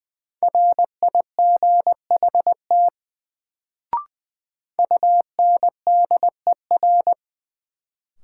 Morse code, heard as a keyed tone, about 21 wpm, 700 Hz. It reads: RIGHT UNDER
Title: Sets of 2 Words - New Word Drill - 20wpm